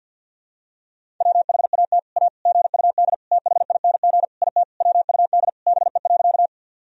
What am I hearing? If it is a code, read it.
Morse code, heard as a keyed tone, about 50 wpm, 700 Hz. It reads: WHAT A GUD THING IT WUD BE.
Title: Sentences from Top 200 Words and Abbreviations 50wpm